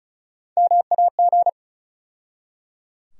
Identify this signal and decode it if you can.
Morse code, heard as a keyed tone, about 34 wpm, 700 Hz. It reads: MAG
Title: USA Call Sign Postfixes - 3 Letters 35wpm